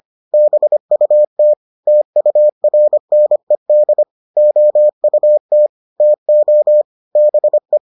Morse code, heard as a keyed tone, about 25 wpm, 600 Hz. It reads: BUT TURNED OUT TO BE